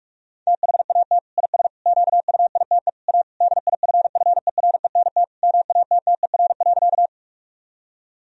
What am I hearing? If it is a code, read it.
Morse code, heard as a keyed tone, about 46 wpm, 700 Hz. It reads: THAT IS QUITE A DIFFERENT MATTER.
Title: Sentences from Top 1000 Words - Review Part 3 - 45wpm